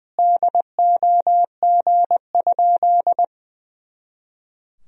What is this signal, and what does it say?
Morse code, heard as a keyed tone, about 20 wpm, 700 Hz. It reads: DOG?
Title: Sentences from Top 400 Words and Abbreviations 20wpm